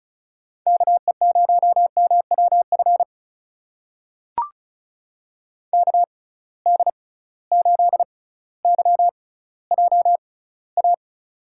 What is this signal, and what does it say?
Morse code, heard as a keyed tone, about 36 wpm, spaced as for 18 wpm, 700 Hz. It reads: KE0MWF KD8YJA
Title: USA Call Signs - 6 Characters - 6x Character Spacing - 35wpm